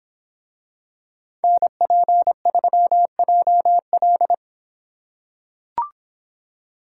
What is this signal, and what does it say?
Morse code, heard as a keyed tone, about 24 wpm, 700 Hz. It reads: NP3JL